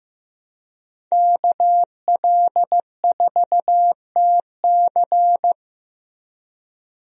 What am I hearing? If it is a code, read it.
Morse code, heard as a keyed tone, about 15 wpm, 700 Hz. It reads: KL4TC
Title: USA Call Signs - 5 Characters - Repeated 3x - 15wpm